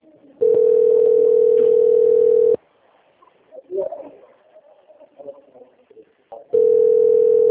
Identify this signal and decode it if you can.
Ringback tone (North America), 440 + 480 Hz, with speech.
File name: out- - - - - .38913